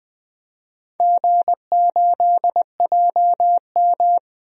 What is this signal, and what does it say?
Morse code, heard as a keyed tone, about 20 wpm, 700 Hz. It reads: G8JM
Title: DXCC-100 Call Signs - 4 Characters - 20wpm